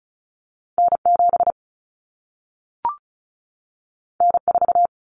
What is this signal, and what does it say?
Morse code, heard as a keyed tone, about 36 wpm, 700 Hz. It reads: N7 N4
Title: Common Cut Numbers - 2-Digit 35wpm